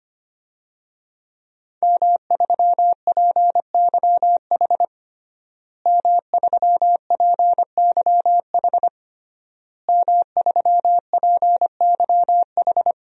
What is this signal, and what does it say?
Morse code, heard as a keyed tone, about 25 wpm, 700 Hz. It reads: M3PYH M3PYH M3PYH
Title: DXCC-100 Call Signs - 5 Characters - Repeated 3x - 25wpm